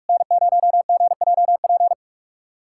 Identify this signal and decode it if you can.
Morse code, heard as a keyed tone, about 46 wpm, 690 Hz. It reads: N0GJP
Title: SOTA QSO Practice 45wpm